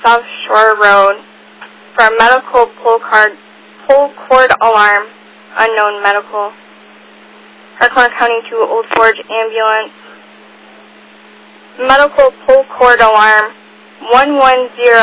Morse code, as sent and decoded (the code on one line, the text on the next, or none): none